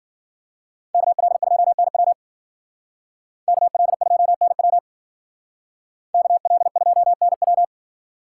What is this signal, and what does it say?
Morse code, heard as a keyed tone, about 52 wpm, 700 Hz. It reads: KL2NW KL2NW KL2NW